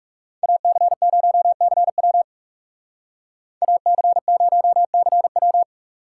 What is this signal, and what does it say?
Morse code, heard as a keyed tone, about 39 wpm, 700 Hz. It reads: AC0CW AC0CW